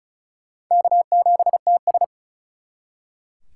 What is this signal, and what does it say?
Morse code, heard as a keyed tone, about 44 wpm, 700 Hz. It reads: K7TS
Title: Speed Racing - USA 1x2 Call Signs 35wpm